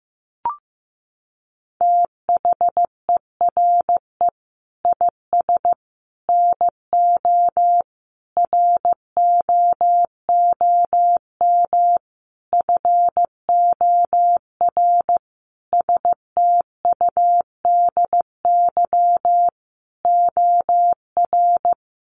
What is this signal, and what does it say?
Morse code, heard as a keyed tone, about 15 wpm, 700 Hz. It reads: THERE IS NO ROOM FOR STUDY OR
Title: Sentences from Top 1200 Words 15wpm